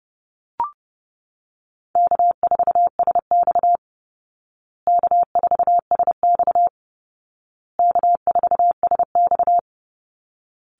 Morse code, heard as a keyed tone, about 30 wpm, 700 Hz. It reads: K4SX K4SX K4SX
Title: Call Signs Repeated 2x - 30wpm